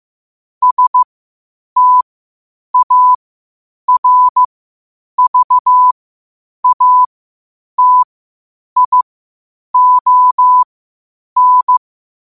Morse code, heard as a keyed tone, about 15 wpm, spaced as for 8 wpm, 1000 Hz. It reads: STARVATION
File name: qotdfiles / 8wpm